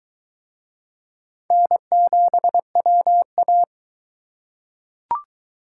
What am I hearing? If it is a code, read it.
Morse code, heard as a keyed tone, about 22 wpm, 700 Hz. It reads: N7WA